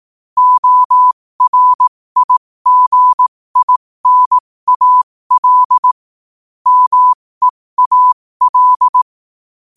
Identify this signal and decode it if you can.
Morse code, heard as a keyed tone, about 18 wpm, spaced as for 15 wpm, 1000 Hz. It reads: ORIGINAL MEAL
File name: qotdfiles / 15wpm